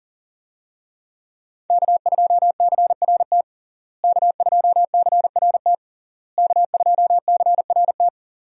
Morse code, heard as a keyed tone, about 40 wpm, 700 Hz. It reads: K2CRT K2CRT K2CRT
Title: USA Call Signs - 5 Characters - Repeated 3x - 40wpm